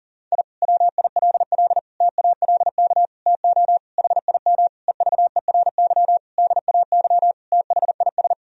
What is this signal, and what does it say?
Morse code, heard as a keyed tone, about 40 wpm, 700 Hz. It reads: I WILL TALK TO HIM EVERY DAY THIS
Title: Sentences from Top 1000 Words - Review Part 2 - 40wpm